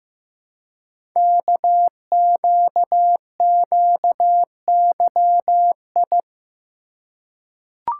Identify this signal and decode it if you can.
Morse code, heard as a keyed tone, about 15 wpm, 700 Hz. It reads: KQQYI
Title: Code Groups - 5 Letter:Numbers - 15wpm